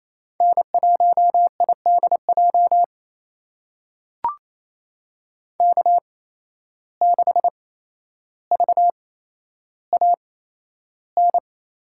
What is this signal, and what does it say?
Morse code, heard as a keyed tone, about 28 wpm, spaced as for 12 wpm, 700 Hz. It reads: N1IDJ K6VAN
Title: USA Call Signs - 5 Characters - 8x Character Spacing - 28wpm